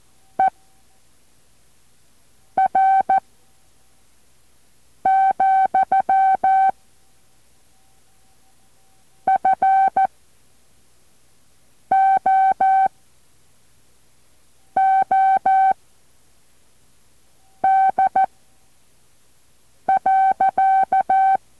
Morse code, heard as a keyed tone, about 14 wpm, 770 Hz. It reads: E R , F O O D .